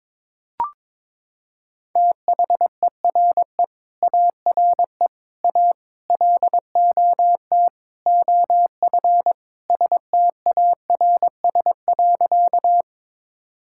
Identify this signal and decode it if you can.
Morse code, heard as a keyed tone, about 22 wpm, 700 Hz. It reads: THERE ARE A LOT OF STARS.
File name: practice / 22wpm / Sentences with 4-Word Phrases - 22wpm